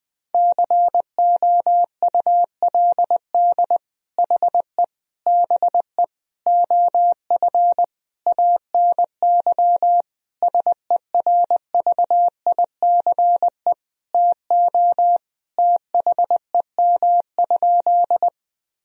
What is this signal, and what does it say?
Morse code, heard as a keyed tone, about 21 wpm, 700 Hz. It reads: COULD HE BE OF ANY SERVICE TO THEM?
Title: Sentences from Top 1000 Words - Review Part 4 - 20wpm